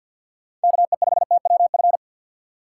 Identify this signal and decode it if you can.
Morse code, heard as a keyed tone, about 52 wpm, 700 Hz. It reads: KE5TWF
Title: USA Call Signs - 6 Characters - 6x Character Spacing - 50wpm